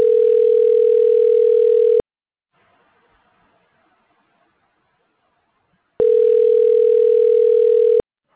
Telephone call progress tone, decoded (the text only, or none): ringback tone